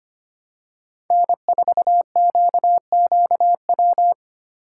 Morse code, heard as a keyed tone, about 25 wpm, 700 Hz. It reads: N4QQW
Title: USA Call Signs - 5 Characters - 3x Character Spacing - 25wpm